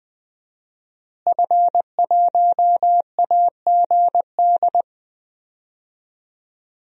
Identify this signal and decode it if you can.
Morse code, heard as a keyed tone, about 20 wpm, 700 Hz. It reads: F1AGD
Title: DXCC-100 Call Signs - 5 Characters - Repeated 3x - 20wpm